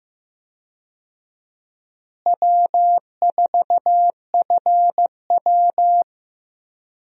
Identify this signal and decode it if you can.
Morse code, heard as a keyed tone, about 15 wpm, 700 Hz. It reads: W4FW
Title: USA Call Signs - 4 Characters - Repeated 2x - 15wpm